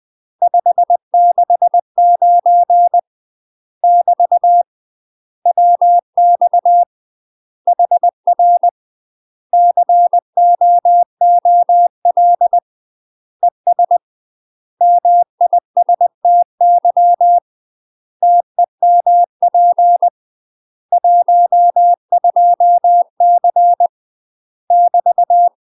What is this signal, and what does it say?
Morse code, heard as a keyed tone, about 20 wpm, 700 Hz. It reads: 569 <BT> WX HR COOL ES MISTY TEMP 12C <BT>
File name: UK MASTER QSOs TRAINING / 20WPM / UK MASTER QSOs 0025